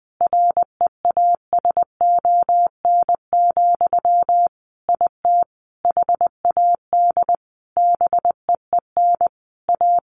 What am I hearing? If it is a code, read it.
Morse code, heard as a keyed tone, about 20 wpm, 700 Hz. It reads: REASON, IT HAD BEEN A
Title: Sentences from Top 1500 Words 20wpm